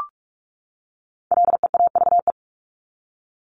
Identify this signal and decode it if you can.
Morse code, heard as a keyed tone, about 44 wpm, 700 Hz. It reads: LEAVE
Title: ING Suffix - Top 100 Words with Base Words 45wpm